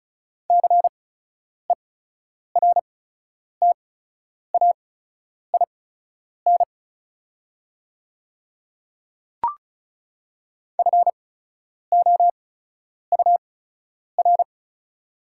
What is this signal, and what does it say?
Morse code, heard as a keyed tone, about 36 wpm, spaced as for 10 wpm, 700 Hz. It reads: CERTAIN FOUR
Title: 8x Character Spacing - Top 500 Words - 35wpm